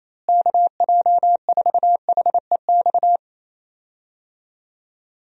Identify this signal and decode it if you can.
Morse code, heard as a keyed tone, about 27 wpm, 700 Hz. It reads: KJ4HEX
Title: USA Call Signs - 6 Characters - Repeated 2x - 28wpm